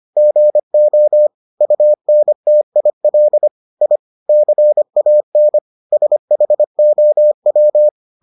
Morse code, heard as a keyed tone, about 25 wpm, 600 Hz. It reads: GO UNTIL I CAN SHOW